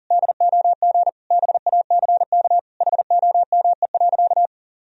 Morse code, heard as a keyed tone, about 37 wpm, 700 Hz. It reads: DOG BACK HOME.
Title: Sentences from Top 1000 Words - Review Part 2 - 40wpm